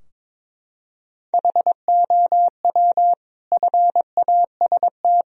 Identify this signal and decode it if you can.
Morse code, heard as a keyed tone, about 21 wpm, 700 Hz. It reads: HOW FAST
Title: Sentences from Top 1000 Words - Review Part 4 - 22wpm